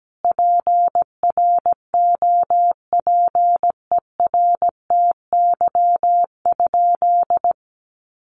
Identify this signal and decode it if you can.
Morse code, heard as a keyed tone, about 17 wpm, 700 Hz. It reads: PROPERTY?
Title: Sentences from Top 1500 Words 17wpm